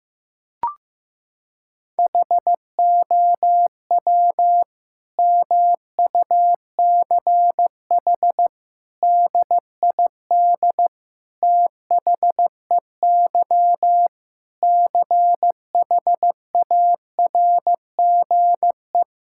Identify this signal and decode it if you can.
Morse code, heard as a keyed tone, about 15 wpm, 700 Hz. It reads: HOW MUCH DID THEY CHARGE
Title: Sentences from Top 1000 Words - Review Part 6 - 15wpm